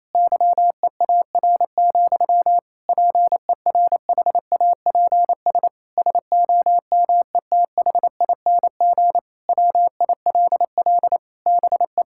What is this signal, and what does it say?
Morse code, heard as a keyed tone, about 28 wpm, 700 Hz. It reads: YEAR, PERHAPS SOMETHING WILL BE
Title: Sentences from Top 1600 Words 28wpm